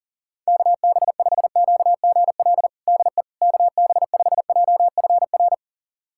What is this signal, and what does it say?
Morse code, heard as a keyed tone, about 40 wpm, 700 Hz. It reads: KB5QGL DE KB5JFR
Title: USA Callsign DE Callsign - 6 Characters - 40wpm